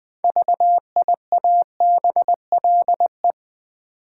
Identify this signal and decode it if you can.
Morse code, heard as a keyed tone, about 20 wpm, 700 Hz. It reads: VIABLE
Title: 6 Letter Words - First 3 Letters Given - 20wpm